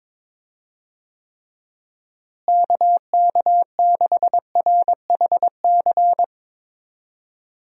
Morse code, heard as a keyed tone, about 22 wpm, 700 Hz. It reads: KK6RHC
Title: USA Call Signs - 6 Characters - Repeated 3x - 22wpm